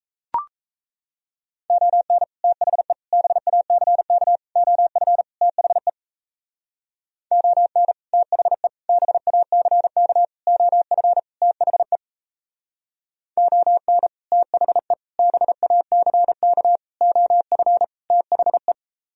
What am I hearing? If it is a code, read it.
Morse code, heard as a keyed tone, about 34 wpm, 700 Hz. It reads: ON THE BACK OF THE ON THE BACK OF THE ON THE BACK OF THE
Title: Speed-Racing - Sets of 5 Words 28wpm